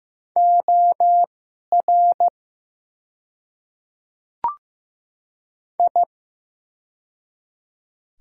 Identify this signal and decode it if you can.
Morse code, heard as a keyed tone, about 15 wpm, spaced as for 11 wpm, 700 Hz. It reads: OR I
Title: 2x Character Spacing - Top 100 Words - 15wpm